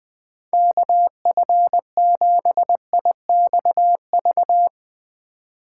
Morse code, heard as a keyed tone, about 20 wpm, 700 Hz. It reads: KF7IXV
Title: Speed Racing - USA 2x3 Call Signs 17wpm